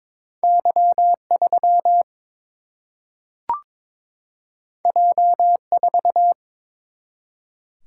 Morse code, heard as a keyed tone, about 22 wpm, 700 Hz. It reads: Y 3 J 4
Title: Single-Letter Single-Number 22wpm